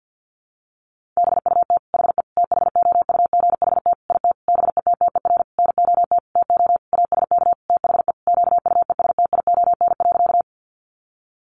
Morse code, heard as a keyed tone, about 50 wpm, 700 Hz. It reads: BUT HE THOUGHT IT BETTER NOT TO ASK THE QUESTION.
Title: Sentences from Top 600 Words 50wpm